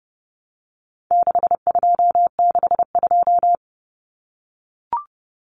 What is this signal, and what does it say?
Morse code, heard as a keyed tone, about 30 wpm, 700 Hz. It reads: 6262
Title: Numbers - 4-Digit 30wpm